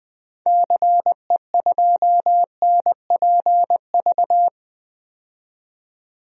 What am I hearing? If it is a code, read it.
Morse code, heard as a keyed tone, about 20 wpm, 700 Hz. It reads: CE2NPV
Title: DXCC-100 Call Signs - 6 Characters - 20wpm